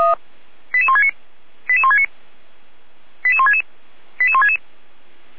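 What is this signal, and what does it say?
Five-tone selective call (ZVEI), heard as quick runs of five tones, tones of about 70 ms. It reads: 88169 88169 88166 88166
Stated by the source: ZVEI five-tone call received on SDR, feuerwehr-orlamuende.org fire brigade.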